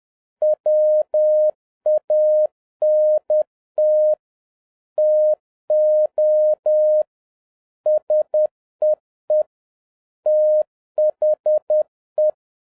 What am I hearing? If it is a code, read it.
Morse code, heard as a keyed tone, about 10 wpm, 610 Hz. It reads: WANT TO SEE THE